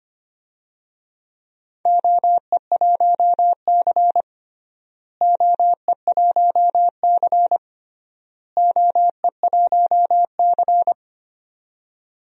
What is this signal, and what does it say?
Morse code, heard as a keyed tone, about 25 wpm, 700 Hz. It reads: OE1C OE1C OE1C
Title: DXCC-100 Call Signs - 4 Characters - Repeated 3x - 25wpm